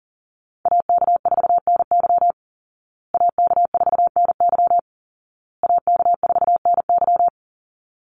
Morse code, heard as a keyed tone, about 40 wpm, 700 Hz. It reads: AK4NY AK4NY AK4NY
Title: Call Signs Repeated 2x - 40wpm